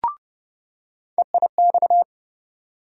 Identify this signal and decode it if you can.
Morse code, heard as a keyed tone, about 30 wpm, 700 Hz. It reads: EIX
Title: USA Call Sign Postfixes - 3 Letters 30wpm